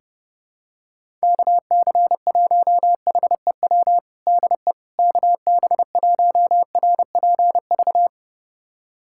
Morse code, heard as a keyed tone, about 30 wpm, 700 Hz. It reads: KC1HEW DE KB1RPV
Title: USA Callsign DE Callsign - 6 Characters - 30wpm